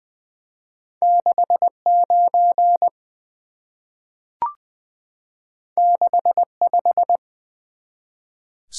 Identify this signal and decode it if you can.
Morse code, heard as a keyed tone, about 21 wpm, 700 Hz. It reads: 69 65
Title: Two Letter-Number Combinations - 20wpm